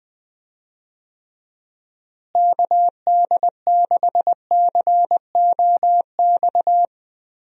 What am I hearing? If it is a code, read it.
Morse code, heard as a keyed tone, about 20 wpm, 700 Hz. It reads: KD6COX
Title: USA Call Signs - 6 Characters - Repeated 2x - 20wpm